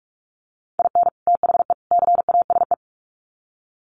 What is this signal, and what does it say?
Morse code, heard as a keyed tone, about 44 wpm, 700 Hz. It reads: IN THE CASE
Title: Sets of 3 Words 45wpm